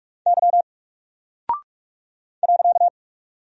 Morse code, heard as a keyed tone, about 44 wpm, 700 Hz. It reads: Y .
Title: Single Character (Letter:Number:Punctuation:Prosign) 45wpm